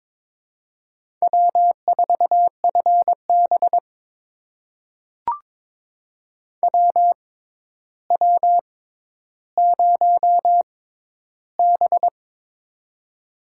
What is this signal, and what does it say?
Morse code, heard as a keyed tone, about 22 wpm, spaced as for 11 wpm, 700 Hz. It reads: W4FB WW0B
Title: USA Call Signs - 4 Characters - 6x Character Spacing - 22wpm